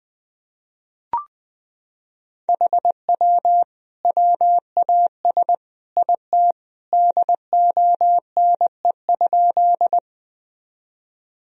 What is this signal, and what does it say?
Morse code, heard as a keyed tone, about 20 wpm, 700 Hz. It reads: HW WAS IT DONE?
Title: Sentences from Top 500 Words and Abbreviations 20wpm